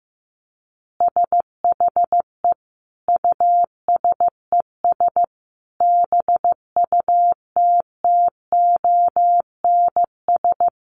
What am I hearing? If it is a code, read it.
Morse code, heard as a keyed tone, about 15 wpm, 700 Hz. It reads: SHE USES BUTTONS
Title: Easy Phrases 15wpm